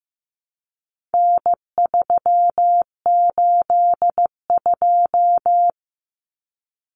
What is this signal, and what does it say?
Morse code, heard as a keyed tone, about 15 wpm, 700 Hz. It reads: N382
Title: Common Cut Numbers - 4-Digit 15wpm